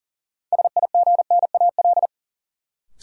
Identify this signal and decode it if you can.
Morse code, heard as a keyed tone, about 40 wpm, 700 Hz. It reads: SIGNAL
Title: Sentences from Top 1400 Words - New Word Drill - 40wpm